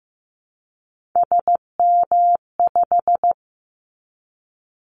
Morse code, heard as a keyed tone, about 15 wpm, 700 Hz. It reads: SM5
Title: Callsign Prefixes for Top 25 DXCC - 15wpm